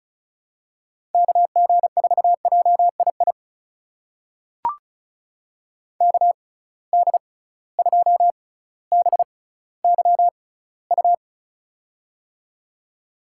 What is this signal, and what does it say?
Morse code, heard as a keyed tone, about 36 wpm, spaced as for 18 wpm, 700 Hz. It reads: KG4JII KD2BYU
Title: USA Call Signs - 6 Characters - 6x Character Spacing - 35wpm